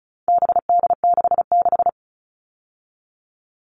Morse code, heard as a keyed tone, about 34 wpm, 700 Hz. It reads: BD66
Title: All Cut Numbers - 4-Digit 35wpm